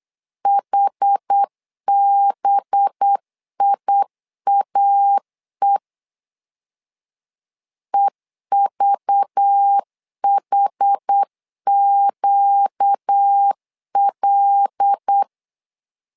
Morse code, heard as a keyed tone, about 8 wpm, 800 Hz. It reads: HBIAE EVHQL
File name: LEZIONI / 5 LBVC / 5 LBV 8wpm 2837